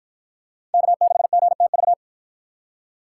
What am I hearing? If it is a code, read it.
Morse code, heard as a keyed tone, about 48 wpm, 700 Hz. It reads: K6GTV